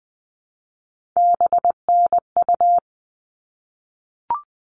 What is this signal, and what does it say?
Morse code, heard as a keyed tone, about 20 wpm, 700 Hz. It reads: BNU
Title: All Cut Numbers - 3-Digit 20wpm